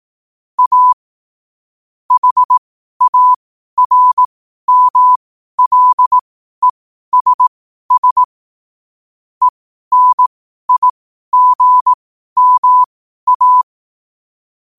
Morse code, heard as a keyed tone, about 18 wpm, spaced as for 12 wpm, 1000 Hz. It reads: A HARMLESS ENIGMA
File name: qotdfiles / 12wpm